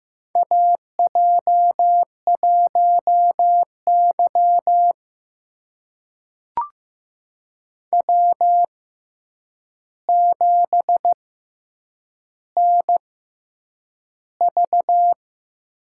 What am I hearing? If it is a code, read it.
Morse code, heard as a keyed tone, about 15 wpm, spaced as for 8 wpm, 700 Hz. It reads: AJ1Y W7NV
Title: USA Call Signs - 4 Characters - 6x Character Spacing - 15wpm